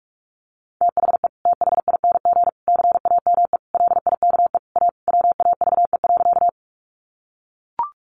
Morse code, heard as a keyed tone, about 44 wpm, 700 Hz. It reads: THE THING CAME LIKE A WAVE.
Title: Sentences from Top 1400 Words 45wpm